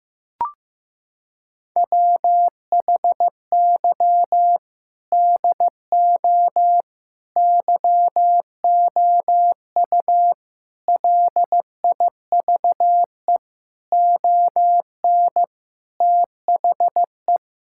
Morse code, heard as a keyed tone, about 15 wpm, 700 Hz. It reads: WHY DO YOU LIVE ON THE
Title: Sentences from Top 1000 Words - Review Part 6 - 15wpm